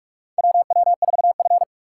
Morse code, heard as a keyed tone, about 46 wpm, 700 Hz. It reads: WW4F
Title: USA Call Signs - 3x Character Spacing - 45wpm